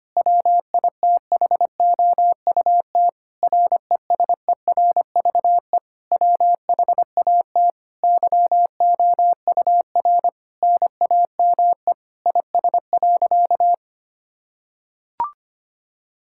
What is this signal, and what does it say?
Morse code, heard as a keyed tone, about 25 wpm, 700 Hz. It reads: WITHOUT RESERVE WHAT YOUR NAME IS.
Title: Sentences from Top 1000 Words - Review Part 6 - 25wpm